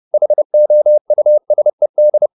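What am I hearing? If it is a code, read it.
Morse code, heard as a keyed tone, about 30 wpm, 600 Hz. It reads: HOUSED
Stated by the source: Morse code at 29 wpm, sending Ken Follett's novel Never in English, from CW med Gnister.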